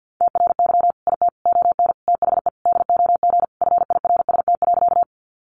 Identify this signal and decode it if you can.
Morse code, heard as a keyed tone, about 50 wpm, 700 Hz. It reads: TRY IT ON THE DOG FIRST.
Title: Sentences from Top 400 Words 50wpm